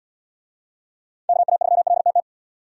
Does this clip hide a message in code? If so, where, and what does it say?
Yes: Morse code from 1.3 to 2.2 s, heard as a keyed tone, about 78 wpm, 700 Hz. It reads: KA3LTT